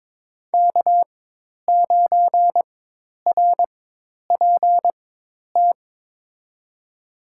Morse code, heard as a keyed tone, about 22 wpm, spaced as for 10 wpm, 700 Hz. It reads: K9RPT
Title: USA Call Signs - 5 Characters - 4x Character Spacing - 22wpm